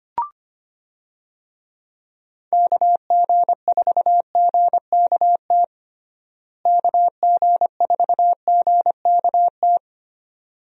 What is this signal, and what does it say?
Morse code, heard as a keyed tone, about 25 wpm, 700 Hz. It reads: KG4GKT KG4GKT
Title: USA Call Signs - 6 Characters - Repeated 2x - 25wpm